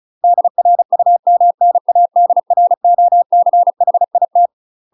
Morse code, heard as a keyed tone, about 36 wpm, 700 Hz. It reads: DRUMNADROCHIT
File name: UK MASTER QSOs TRAINING / 35WPM / UK MASTER QSOs 0029